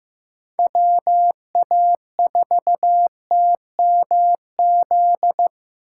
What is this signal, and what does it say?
Morse code, heard as a keyed tone, about 15 wpm, 700 Hz. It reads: WA4TMZ
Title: USA Call Signs - 8x Character Spacing - 15wpm